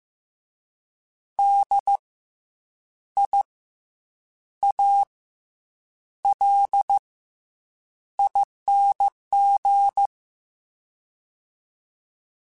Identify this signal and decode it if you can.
Morse code, heard as a keyed tone, about 15 wpm, spaced as for 7 wpm, 800 Hz. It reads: DIALING
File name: Special INGs-mp3s / Special INGs-mp3s / Fast ing @13